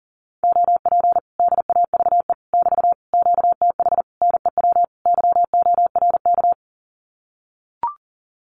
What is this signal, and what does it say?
Morse code, heard as a keyed tone, about 40 wpm, 700 Hz. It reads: OP DAVE <BT> QTH NEW YORK
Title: Common QSO Phrases Encore 40wpm